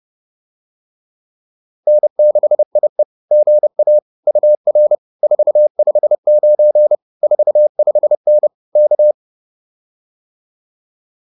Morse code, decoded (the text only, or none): N6IE GA UR 459 45N K